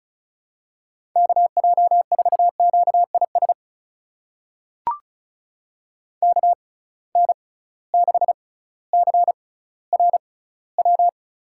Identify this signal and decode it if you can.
Morse code, heard as a keyed tone, about 36 wpm, spaced as for 18 wpm, 700 Hz. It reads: KJ4QIS KN6CRW
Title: USA Call Signs - 6 Characters - 6x Character Spacing - 35wpm